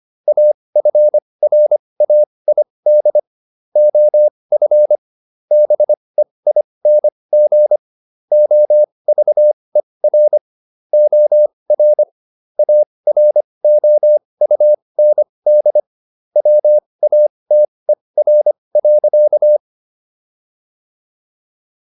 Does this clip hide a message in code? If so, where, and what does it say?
Yes: Morse code from 0.3 to 19.6 s, heard as a keyed tone, about 25 wpm, spaced as for 20 wpm, 600 Hz. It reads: AFRAID OF BEING OVER OR AROUND WATER.